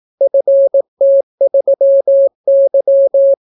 Morse code, heard as a keyed tone, about 18 wpm, 550 Hz. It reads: FT3Y